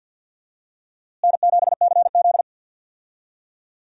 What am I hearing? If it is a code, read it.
Morse code, heard as a keyed tone, about 46 wpm, 700 Hz. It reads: N7CZ
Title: USA Call Signs - 4 Characters - Repeated 3x - 50wpm